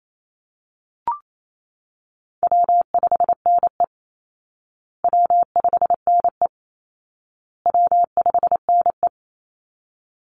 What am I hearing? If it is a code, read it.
Morse code, heard as a keyed tone, about 28 wpm, 700 Hz. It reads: W5NE W5NE W5NE
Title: Call Signs Repeated 2x - 28wpm